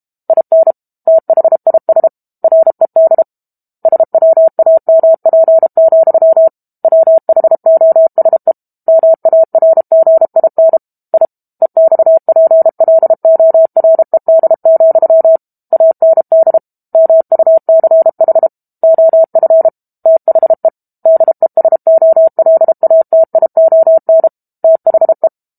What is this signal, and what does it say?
Morse code, heard as a keyed tone, about 32 wpm, 650 Hz. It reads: IN THIS RED SWAMP, WHOSE MARGIN I EXPLORED, AND MUCH OF THE DESOLATION THE
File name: Book Two / 06-Chapter 6 - 32 WPM